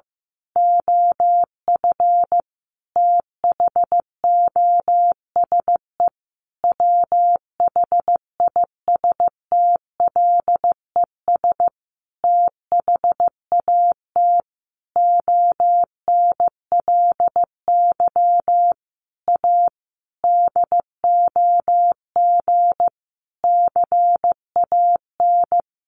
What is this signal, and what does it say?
Morse code, heard as a keyed tone, about 15 wpm, 700 Hz. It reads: OF THOSE WHISTLES THAT ONLY A DOG CAN
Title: A Sound Of Thunder 15wpm